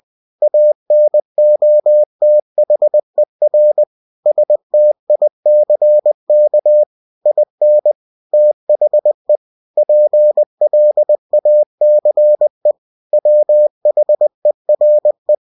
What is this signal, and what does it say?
Morse code, heard as a keyed tone, about 20 wpm, 600 Hz. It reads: ANOTHER STICK IN THE PLACE WHERE
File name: cw-books / winnie-the-pooh / winnie-20wpm-page-0096